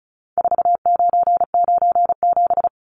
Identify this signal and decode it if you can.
Morse code, heard as a keyed tone, about 34 wpm, 700 Hz. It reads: 4997
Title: Numbers - 4-Digit 35wpm